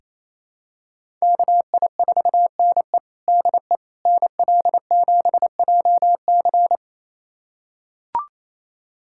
Morse code, heard as a keyed tone, about 28 wpm, 700 Hz. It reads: KI4NE DE NL7JC
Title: USA Callsign DE Callsign 28wpm